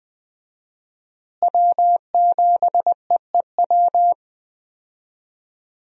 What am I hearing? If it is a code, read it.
Morse code, heard as a keyed tone, about 20 wpm, 700 Hz. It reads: W7EEW
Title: USA Call Signs - 5 Characters - Repeated 2x - 20wpm